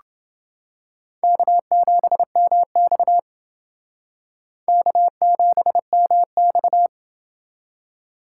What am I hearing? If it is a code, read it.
Morse code, heard as a keyed tone, about 29 wpm, 700 Hz. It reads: K7MX K7MX